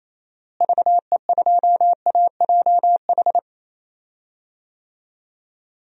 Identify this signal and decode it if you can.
Morse code, heard as a keyed tone, about 28 wpm, 700 Hz. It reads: VE2AJH
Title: DXCC-100 Call Signs - 6 Characters - Repeated 2x - 28wpm